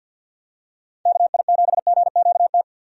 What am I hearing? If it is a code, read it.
Morse code, heard as a keyed tone, about 50 wpm, 700 Hz. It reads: KI7GQT
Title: USA Call Signs - 6 Characters - 50wpm